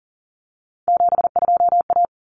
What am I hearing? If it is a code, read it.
Morse code, heard as a keyed tone, about 40 wpm, 700 Hz. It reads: 72A
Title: Common Cut Numbers - 3-Digit 40wpm